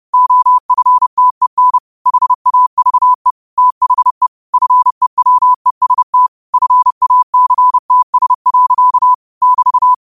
Morse code, heard as a keyed tone, about 30 wpm, 1000 Hz. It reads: OFTEN HAVE THE FEWEST FACTS. <BT>
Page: Quotes for Wed, 13 Aug 2025 in Morse Code at 30 words per minute.